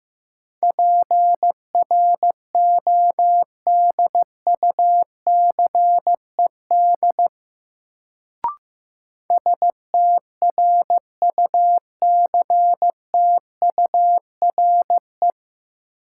Morse code, heard as a keyed tone, about 15 wpm, 700 Hz. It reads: PRODUCED STRUCTURE
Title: Sentences from Top 1000 Words - Review Part 6 - New Word Drill - 15wpm